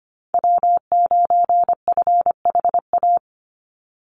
Morse code, heard as a keyed tone, about 25 wpm, 700 Hz. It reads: W9FHA
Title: NA SOTA Chaser Callsigns 25wpm